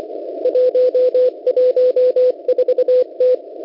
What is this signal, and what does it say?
Morse code, heard as a keyed tone, about 25 wpm, 500 Hz. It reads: 114T
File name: Unid- -0700utc-4735khz